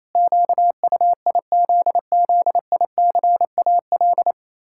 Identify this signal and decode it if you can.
Morse code, heard as a keyed tone, about 28 wpm, 700 Hz. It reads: QUIZZICAL
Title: Pangram Challenge v2 - 28wpm